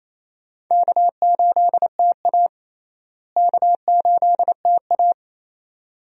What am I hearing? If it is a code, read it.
Morse code, heard as a keyed tone, about 28 wpm, 700 Hz. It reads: K8TA K8TA
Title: USA Call Signs - 4 Characters - Repeated 2x - 28wpm